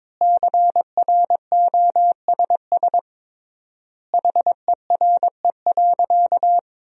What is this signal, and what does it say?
Morse code, heard as a keyed tone, about 22 wpm, 700 Hz. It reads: CROSS HERE.
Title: 3x Word-Spacing - Sentences from Top 400 Words - 22wpm